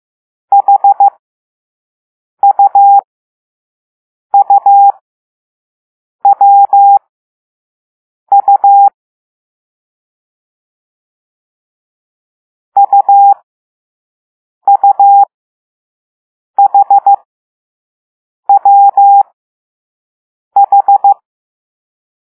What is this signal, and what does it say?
Morse code, heard as a keyed tone, about 15 wpm, spaced as for 5 wpm, 800 Hz. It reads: HUUWU UUHWH